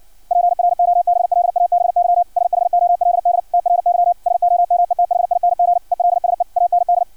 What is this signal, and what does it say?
Morse code, heard as a keyed tone, about 56 wpm, 700 Hz. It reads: PROBLABY SHOLD TRY SOMETHING ELSE AND
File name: archived / CRSnet